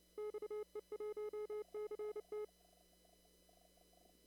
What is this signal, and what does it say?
Morse code, heard as a keyed tone, about 30 wpm, 420 Hz. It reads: XE1CT
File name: CW / kt5j wpx cw-op-k5pi